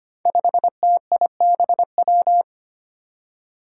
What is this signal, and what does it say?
Morse code, heard as a keyed tone, about 25 wpm, 700 Hz. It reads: 5TIBW
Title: Five Challenging Letter-Number Combinations 25wpm